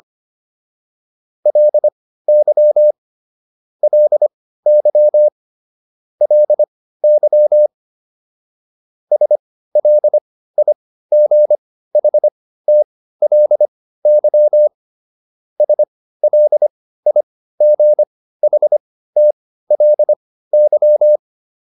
Morse code, heard as a keyed tone, about 25 wpm, spaced as for 15 wpm, 600 Hz. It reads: LY LY LY SLIGHTLY SLIGHTLY